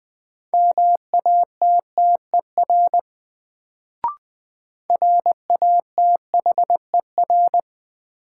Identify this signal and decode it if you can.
Morse code, heard as a keyed tone, about 20 wpm, 700 Hz. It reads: MATTER RATHER
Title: Sentences from Top 700 Words - New Word Drill - 20wpm